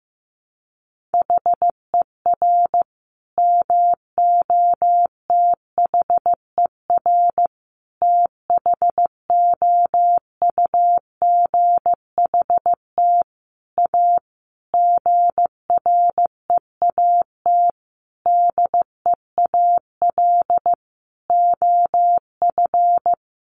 Morse code, heard as a keyed tone, about 15 wpm, 700 Hz. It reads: HER MOTHER THOUGHT A GREAT DEAL OF
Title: Sentences from Top 800 Words 15wpm